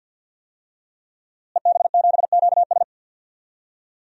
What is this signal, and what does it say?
Morse code, heard as a keyed tone, about 50 wpm, 700 Hz. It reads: EB7QS
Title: DXCC-100 Call Signs - 5 Characters - Repeated 3x - 50wpm